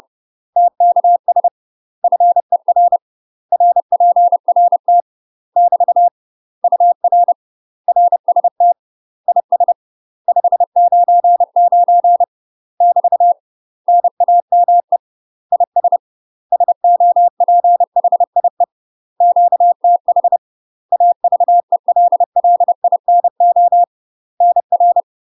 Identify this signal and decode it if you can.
Morse code, heard as a keyed tone, about 30 wpm, 700 Hz. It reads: TKS FER RPRT <BT> UR RST IS 599 <BT> NAME IS SOPHIE QTH AVELLINO NR
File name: UK MASTER QSOs TRAINING/30WPM